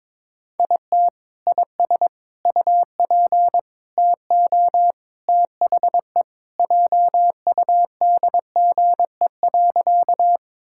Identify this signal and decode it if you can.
Morse code, heard as a keyed tone, about 23 wpm, 700 Hz. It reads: IT IS UP TO THE JUDGE.
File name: practice / 22wpm / Sentences with 3-Word Phrases - 22wpm